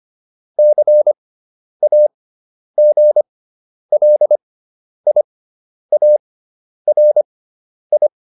Morse code, heard as a keyed tone, about 25 wpm, spaced as for 10 wpm, 600 Hz. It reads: CAGLIARI
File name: QSO409 10